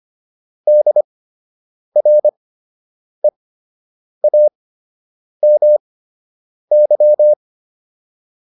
Morse code, heard as a keyed tone, about 25 wpm, spaced as for 8 wpm, 600 Hz. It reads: DREAMY